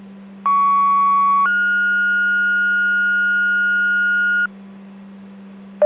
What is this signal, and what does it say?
Two-tone page, heard as a one-second tone, then a three-second tone, about 1120 then 1430 Hz, with noise. It is heard from 0.5 to 4.5 s.